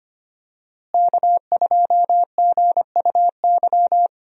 Morse code, heard as a keyed tone, about 25 wpm, 700 Hz. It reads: K2GUY
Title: USA Call Signs - 5 Characters - Repeated 3x - 25wpm